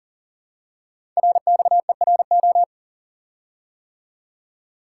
Morse code, heard as a keyed tone, about 42 wpm, 700 Hz. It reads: RXERO
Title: Code Groups - 5 Letters - 40wpm